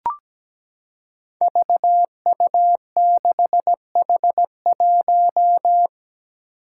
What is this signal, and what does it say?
Morse code, heard as a keyed tone, about 17 wpm, 700 Hz. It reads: VU6H1
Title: Five Challenging Letter-Number Combinations 17wpm